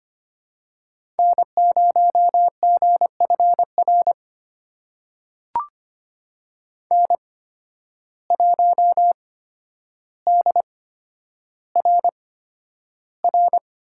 Morse code, heard as a keyed tone, about 25 wpm, spaced as for 11 wpm, 700 Hz. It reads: N0GFR N1DRR
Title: USA Call Signs - 5 Characters - 8x Character Spacing - 25wpm